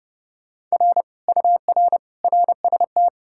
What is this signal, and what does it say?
Morse code, heard as a keyed tone, about 40 wpm, 700 Hz. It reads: R UR RST